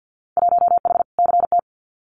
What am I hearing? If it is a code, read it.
Morse code, heard as a keyed tone, about 50 wpm, 700 Hz. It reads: 1H CT
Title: Winter Field Day Exchanges 50wpm